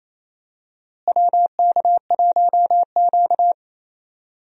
Morse code, heard as a keyed tone, about 28 wpm, 700 Hz. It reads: WK1Q
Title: USA Call Signs - 4 Characters - Repeated 2x - 28wpm